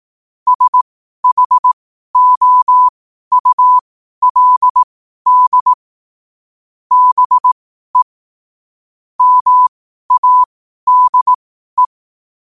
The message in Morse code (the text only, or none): SHOULD BE MADE